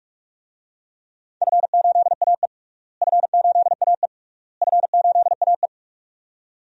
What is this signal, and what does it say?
Morse code, heard as a keyed tone, about 41 wpm, 700 Hz. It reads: F8AE F8AE F8AE
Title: DXCC-100 Call Signs - 4 Characters - Repeated 3x - 45wpm